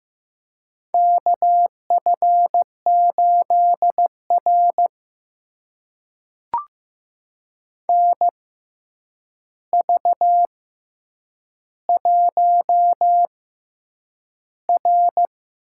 Morse code, heard as a keyed tone, about 15 wpm, spaced as for 8 wpm, 700 Hz. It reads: KF8R NV1R
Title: USA Call Signs - 4 Characters - 6x Character Spacing - 15wpm